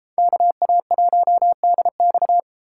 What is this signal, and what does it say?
Morse code, heard as a keyed tone, about 32 wpm, 700 Hz. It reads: KA1DX